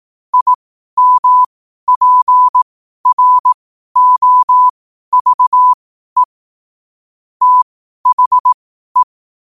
Morse code, heard as a keyed tone, about 18 wpm, spaced as for 12 wpm, 1000 Hz. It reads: IMPROVE THE
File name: qotdfiles / 12wpm